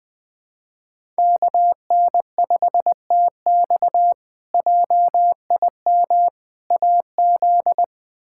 Morse code, heard as a keyed tone, about 20 wpm, 700 Hz. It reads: KN5TX JIM AZ
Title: K1USN SST QSO Practice - 20wpm